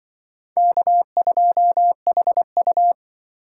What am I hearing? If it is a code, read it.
Morse code, heard as a keyed tone, about 23 wpm, 700 Hz. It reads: K2HU
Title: Speed Racing - USA 1x2 Call Signs 20wpm